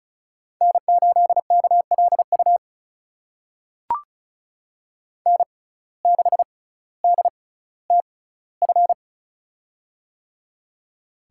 Morse code, heard as a keyed tone, about 36 wpm, spaced as for 18 wpm, 700 Hz. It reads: N8KLU N6DTF
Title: USA Call Signs - 5 Characters - 6x Character Spacing - 35wpm